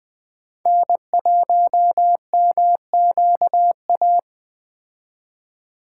Morse code, heard as a keyed tone, about 19 wpm, 700 Hz. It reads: N1MQA